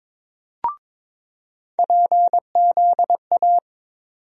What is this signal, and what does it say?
Morse code, heard as a keyed tone, about 22 wpm, 700 Hz. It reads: PZA
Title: USA Call Sign Postfixes - 3 Letters 22wpm